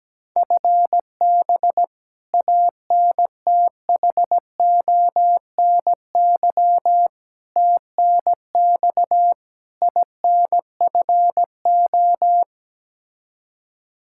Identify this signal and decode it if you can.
Morse code, heard as a keyed tone, about 17 wpm, 700 Hz. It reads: FB ANTHONY TNX INFO
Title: K1USN SST QSO Practice 17wpm